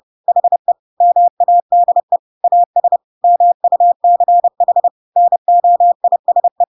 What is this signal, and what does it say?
Morse code, heard as a keyed tone, about 30 wpm, 700 Hz. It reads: HE MADE AS MUCH NOISE
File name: Pinocchio / Pinocchio ch7 30 30 ik1hge